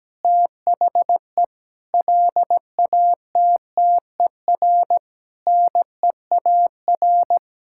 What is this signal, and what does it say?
Morse code, heard as a keyed tone, about 17 wpm, 700 Hz. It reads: THE LATTER NEAR
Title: Sentences from Top 1600 Words 17wpm